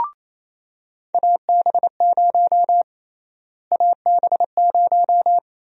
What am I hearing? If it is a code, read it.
Morse code, heard as a keyed tone, about 28 wpm, 700 Hz. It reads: AB0 AB0
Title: USA Call Sign Prefixes - Repeated 2x - 28wpm